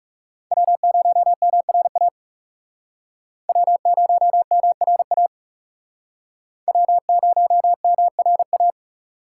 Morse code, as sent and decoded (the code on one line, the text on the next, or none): .-- ----- -- .-. .- / .-- ----- -- .-. .- / .-- ----- -- .-. .-
W0MRA W0MRA W0MRA